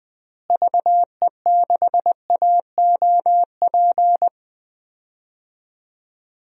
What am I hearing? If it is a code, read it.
Morse code, heard as a keyed tone, about 20 wpm, 700 Hz. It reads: VE6AOP
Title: DXCC-100 Call Signs - 6 Characters - Repeated 2x - 20wpm